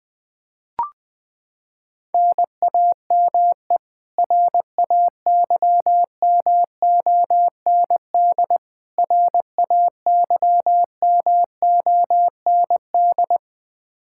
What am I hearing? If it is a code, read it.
Morse code, heard as a keyed tone, about 21 wpm, 700 Hz. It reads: NAME RAYMOND RAYMOND
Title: Practice QSO - Exchange 1 - Beginning - 20wpm